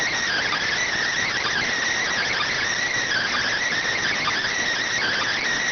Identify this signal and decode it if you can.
Emergency Alert System (SAME) tones, heard as harsh buzzing bursts. It is heard from 1.2 to 5.4 s.